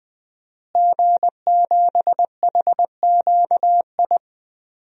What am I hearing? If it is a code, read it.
Morse code, heard as a keyed tone, about 21 wpm, 700 Hz. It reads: G7HQI
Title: DXCC-100 Call Signs - 5 Characters - Repeated 2x - 20wpm